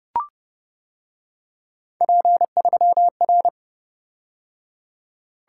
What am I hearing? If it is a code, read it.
Morse code, heard as a keyed tone, about 28 wpm, 700 Hz. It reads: P3R